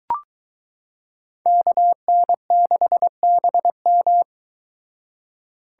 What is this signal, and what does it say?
Morse code, heard as a keyed tone, about 23 wpm, 700 Hz. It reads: KN6BM